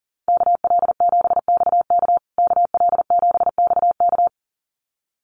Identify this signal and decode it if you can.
Morse code, heard as a keyed tone, about 40 wpm, 700 Hz. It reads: KL7XK KL7XK
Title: USA Call Signs Repeated 2x 40wpm